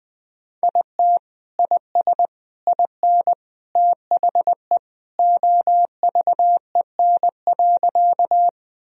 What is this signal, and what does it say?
Morse code, heard as a keyed tone, about 20 wpm, 700 Hz. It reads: IT IS IN THE OVEN.
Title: Sentences from Top 1000 Words - Review Part 4 - 20wpm